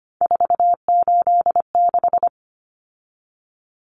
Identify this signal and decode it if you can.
Morse code, heard as a keyed tone, about 25 wpm, 700 Hz. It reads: 486
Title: Common Cut Numbers - 3-Digit 25wpm